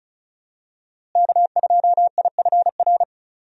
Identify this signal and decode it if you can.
Morse code, heard as a keyed tone, about 35 wpm, 700 Hz. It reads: K2IFR
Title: USA Call Signs - 5 Characters - Repeated 3x - 35wpm